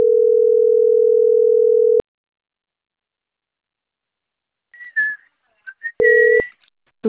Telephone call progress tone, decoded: ringback tone

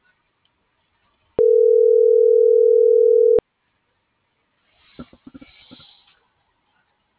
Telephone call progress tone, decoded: ringback tone

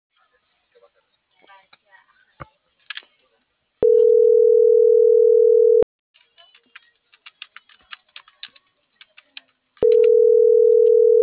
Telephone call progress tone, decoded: ringback tone

